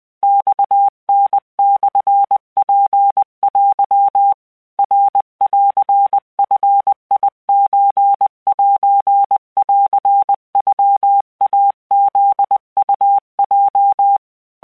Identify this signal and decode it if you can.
Morse code (Japanese Wabun code), heard as a keyed tone, about 20 wpm, 800 Hz. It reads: マタモツテ ナンヂソセンノイフウヲ